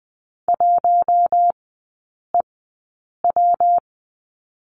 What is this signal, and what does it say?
Morse code, heard as a keyed tone, about 20 wpm, 700 Hz. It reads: 1 E W
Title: 133 Lesson 22e - Question Mark - Three Characters 20wpm